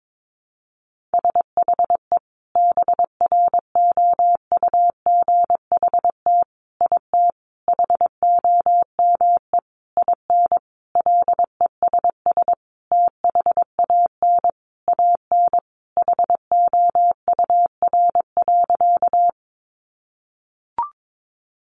Morse code, heard as a keyed tone, about 22 wpm, 700 Hz. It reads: SHE BROUGHT IT HOME IN LESS THAN AN HOUR.
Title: Sentences from Top 500 Words 22wpm